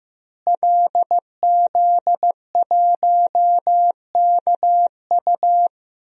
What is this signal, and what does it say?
Morse code, heard as a keyed tone, about 15 wpm, 700 Hz. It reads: LZ1KU
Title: DXCC-100 Call Signs - 5 Characters - Repeated 3x - 15wpm